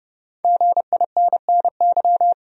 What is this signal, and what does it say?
Morse code, heard as a keyed tone, about 30 wpm, 700 Hz. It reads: GINNY
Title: USA Common Names - Max 5 Letters - 30wpm